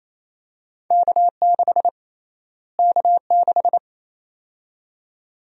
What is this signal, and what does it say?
Morse code, heard as a keyed tone, about 28 wpm, 700 Hz. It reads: K6 K6
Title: USA Call Sign Prefixes - Repeated 2x - 28wpm